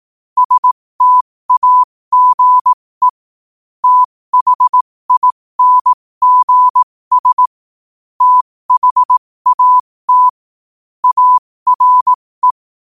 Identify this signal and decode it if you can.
Morse code, heard as a keyed tone, about 18 wpm, spaced as for 15 wpm, 1000 Hz. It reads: STAGE THINGS THAT ARE
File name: qotdfiles / 15wpm